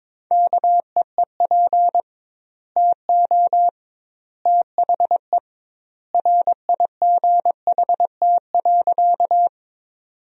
Morse code, heard as a keyed tone, about 22 wpm, 700 Hz. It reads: KEEP TO THE RIGHT.
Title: 2x Word-Spacing - Sentences from Top 300 Words - 22wpm